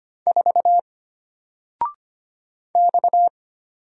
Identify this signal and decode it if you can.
Morse code, heard as a keyed tone, about 25 wpm, 700 Hz. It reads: 4 X
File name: Single-Letter-Number-25wpm